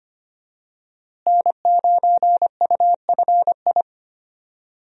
Morse code, heard as a keyed tone, about 25 wpm, 700 Hz. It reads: N9UFI
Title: USA Call Signs - 5 Characters - Repeated 3x - 25wpm